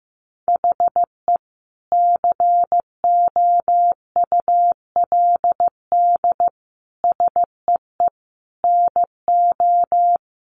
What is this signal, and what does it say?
Morse code, heard as a keyed tone, about 15 wpm, 700 Hz. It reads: HE COULD SEE NO
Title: Sentences from Top 1200 Words 15wpm